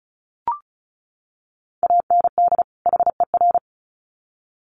Morse code, heard as a keyed tone, about 34 wpm, 700 Hz. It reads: AND HER
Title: Sets of 2 Words 35wpm